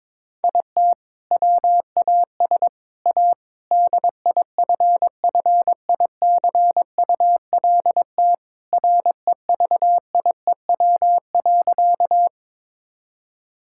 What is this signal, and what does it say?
Morse code, heard as a keyed tone, about 22 wpm, 700 Hz. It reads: IT WAS A DIFFICULT REVIEW.
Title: Sentences from Top 1000 Words - Review Part 6 - 22wpm